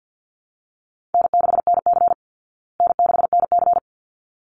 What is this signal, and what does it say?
Morse code, heard as a keyed tone, about 49 wpm, 700 Hz. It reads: N6NC N6NC
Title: Easy Call Signs Repeated 2x - 50wpm